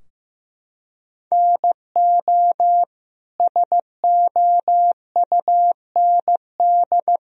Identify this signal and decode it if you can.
Morse code, heard as a keyed tone, about 16 wpm, 700 Hz. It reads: NO SOUND
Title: Sentences from Top 1000 Words - Review Part 4 - 15wpm